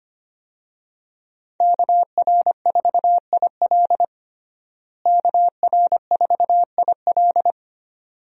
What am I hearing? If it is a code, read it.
Morse code, heard as a keyed tone, about 25 wpm, 700 Hz. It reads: KR4IL KR4IL
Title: USA Call Signs - 5 Characters - Repeated 2x - 25wpm